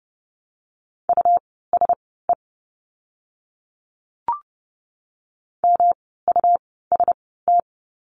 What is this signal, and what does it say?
Morse code, heard as a keyed tone, about 30 wpm, spaced as for 17 wpm, 700 Hz. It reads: USE MUST
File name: course / 30:10wpm / 237 Lesson 38f - Letter q - Words 30wpm